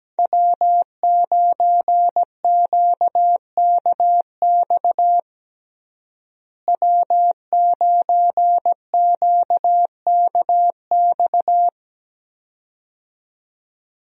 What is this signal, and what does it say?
Morse code, heard as a keyed tone, about 17 wpm, 700 Hz. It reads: W9QKX W9QKX
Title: USA Call Signs - 5 Characters - Repeated 2x - 17wpm